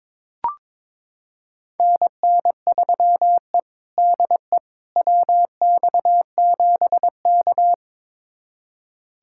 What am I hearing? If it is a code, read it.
Morse code, heard as a keyed tone, about 22 wpm, 700 Hz. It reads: NN3E DE WX7K
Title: USA Callsign DE Callsign - 4 Characters - 22wpm